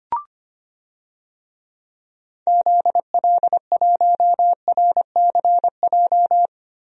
Morse code, heard as a keyed tone, about 25 wpm, 700 Hz. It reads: ZL1RCJ
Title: DXCC-100 Call Signs - 6 Characters - Repeated 2x - 25wpm